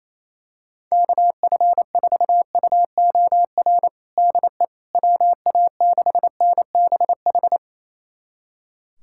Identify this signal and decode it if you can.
Morse code, heard as a keyed tone, about 28 wpm, 700 Hz. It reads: KF4UOR DE WA6NBH
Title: USA Callsign DE Callsign - 6 Characters - 28wpm